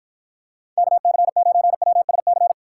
Morse code, heard as a keyed tone, about 52 wpm, 700 Hz. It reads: KC9WSC